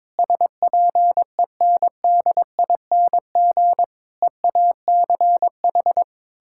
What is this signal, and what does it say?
Morse code, heard as a keyed tone, about 22 wpm, 700 Hz. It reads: SPENDING EACH
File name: Difficult Sentences with Hints - 22wpm